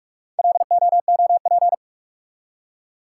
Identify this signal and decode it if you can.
Morse code, heard as a keyed tone, about 46 wpm, 700 Hz. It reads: LOOP
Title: Top 180 QSO Elements 45wpm